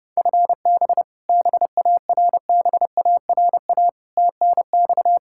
Morse code, heard as a keyed tone, about 30 wpm, 700 Hz. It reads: FB BARBARA TNX
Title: Practice QSO - Exchange 3 - More Info - 30wpm